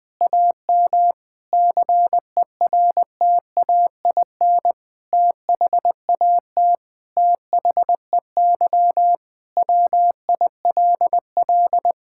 Morse code, heard as a keyed tone, about 20 wpm, 700 Hz. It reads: AM CERTAIN THAT THEY WILL
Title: Sentences from Top 1000 Words - Review Part 6 - 20wpm